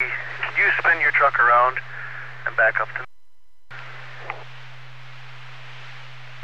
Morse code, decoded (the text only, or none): none